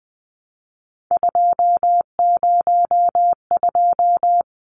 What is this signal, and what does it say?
Morse code, heard as a keyed tone, about 20 wpm, 700 Hz. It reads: 202
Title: Numbers - 3-Digit 20wpm